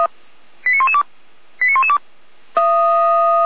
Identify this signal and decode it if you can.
Five-tone selective call (ZVEI), heard as quick runs of five tones, tones of about 80 ms. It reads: 89112 89112